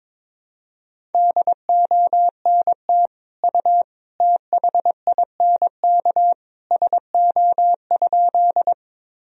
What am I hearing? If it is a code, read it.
Morse code, heard as a keyed tone, about 22 wpm, 700 Hz. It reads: DONT U THINK SO?
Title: Sentences from Top 300 Words and Abbreviations 22wpm